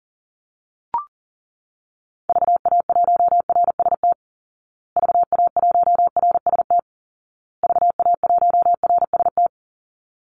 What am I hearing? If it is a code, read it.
Morse code, heard as a keyed tone, about 40 wpm, 700 Hz. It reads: VA1RST VA1RST VA1RST
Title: Call Signs Repeated 2x - 40wpm